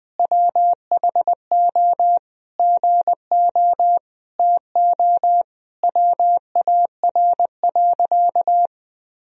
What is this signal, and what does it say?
Morse code, heard as a keyed tone, about 21 wpm, 700 Hz. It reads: WHO GO TO WAR.
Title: Sentences from Top 1000 Words - Review Part 2 - 20wpm